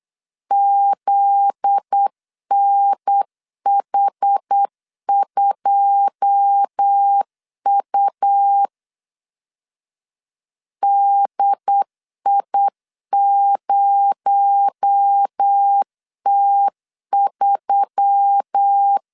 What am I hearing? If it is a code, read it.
Morse code, heard as a keyed tone, about 8 wpm, 800 Hz. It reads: ZNH2U DI0T3
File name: CW / DET / 1 9 1500C S A 8wpm 5637